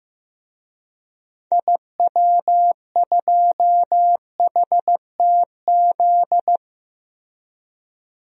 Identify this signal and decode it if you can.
Morse code, heard as a keyed tone, about 15 wpm, 700 Hz. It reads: IW2HTZ
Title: DXCC-100 Call Signs - 6 Characters - Repeated 3x - 15wpm